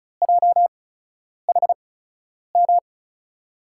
Morse code, heard as a keyed tone, about 36 wpm, spaced as for 10 wpm, 700 Hz. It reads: JHM
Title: USA Call Signs - 6 Characters - 8x Character Spacing - 35wpm